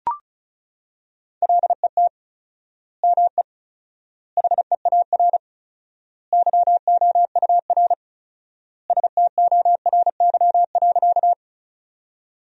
Morse code, heard as a keyed tone, about 36 wpm, 700 Hz. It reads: LET ME HEAR YOUR STORY.
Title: 4x Word-Spacing - Sentences from Top 300 Words - 35wpm